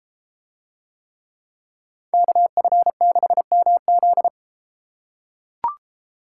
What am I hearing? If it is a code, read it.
Morse code, heard as a keyed tone, about 30 wpm, 700 Hz. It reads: KF6MZ